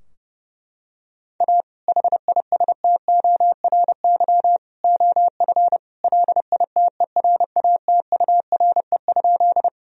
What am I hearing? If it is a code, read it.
Morse code, heard as a keyed tone, about 29 wpm, 700 Hz. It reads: A HISTORY OF LITERATURE?
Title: Sentences from Top 1000 Words - Review Part 3 - 30wpm